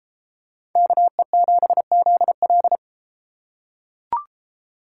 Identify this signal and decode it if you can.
Morse code, heard as a keyed tone, about 33 wpm, 700 Hz. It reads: KE7ZL